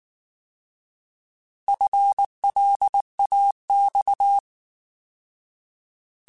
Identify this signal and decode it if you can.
Morse code, heard as a keyed tone, about 19 wpm, 800 Hz. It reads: FLAX
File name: Alphabetical / Alphabetical mp3s / Fs @18